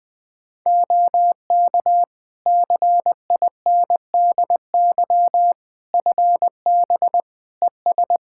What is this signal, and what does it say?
Morse code, heard as a keyed tone, about 20 wpm, 700 Hz. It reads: OK CINDY FB ES
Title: K1USN SST QSO Practice 20wpm